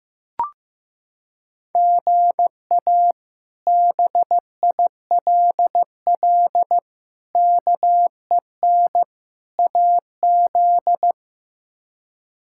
Morse code, heard as a keyed tone, about 15 wpm, 700 Hz. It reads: GA BILL KEN AZ
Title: K1USN SST QSO Practice - 15wpm